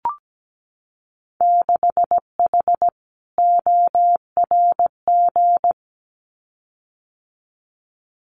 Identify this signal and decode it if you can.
Morse code, heard as a keyed tone, about 17 wpm, 700 Hz. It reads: 6H ORG
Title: Winter Field Day Exchanges Encore 17wpm